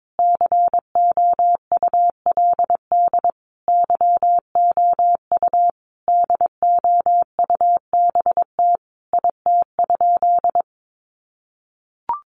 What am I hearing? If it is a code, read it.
Morse code, heard as a keyed tone, about 22 wpm, 700 Hz. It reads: COULD YOU DOUBT IT?
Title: Sentences from Top 1500 Words 22wpm